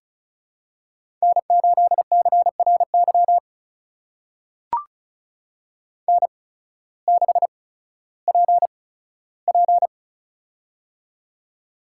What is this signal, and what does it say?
Morse code, heard as a keyed tone, about 36 wpm, spaced as for 16 wpm, 700 Hz. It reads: N8CRY N6PP
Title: USA Call Signs - 8x Character Spacing - 35wpm